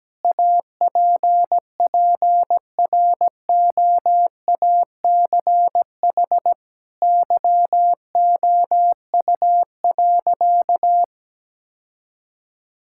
Morse code, heard as a keyed tone, about 17 wpm, 700 Hz. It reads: APPROACH YOU.
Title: Sentences from Top 1000 Words - Review Part 6 - 17wpm